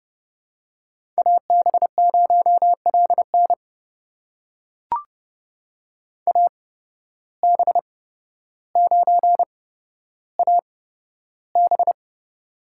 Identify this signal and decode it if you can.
Morse code, heard as a keyed tone, about 31 wpm, spaced as for 13 wpm, 700 Hz. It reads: AB0LN AB9AB
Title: USA Call Signs - 5 Characters - 8x Character Spacing - 30wpm